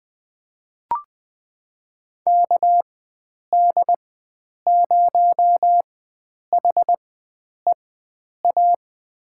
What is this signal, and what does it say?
Morse code, heard as a keyed tone, about 20 wpm, spaced as for 9 wpm, 700 Hz. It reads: KD0HEA
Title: USA Call Signs - 6 Characters - 4x Character Spacing - 20wpm